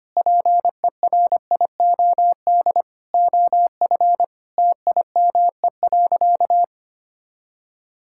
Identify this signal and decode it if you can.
Morse code, heard as a keyed tone, about 25 wpm, 700 Hz. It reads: PERIOD OF TIME.
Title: Sentences from Top 1000 Words - Review Part 5 - 25wpm